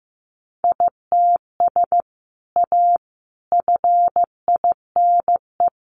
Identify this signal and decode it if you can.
Morse code, heard as a keyed tone, about 15 wpm, 700 Hz. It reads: ITS A FINE
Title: Sentences from Top 600 Words 15wpm